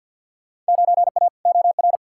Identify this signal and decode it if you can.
Morse code, heard as a keyed tone, about 49 wpm, 700 Hz. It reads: 9A OR
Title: Field Day Exchanges Encore - 50wpm